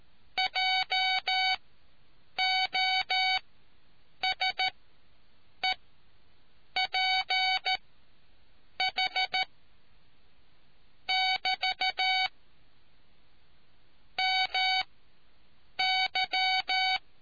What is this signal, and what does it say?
Morse code, heard as a keyed tone, about 13 wpm, spaced as for 7 wpm, 750 Hz. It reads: JOSEPH <BT> MY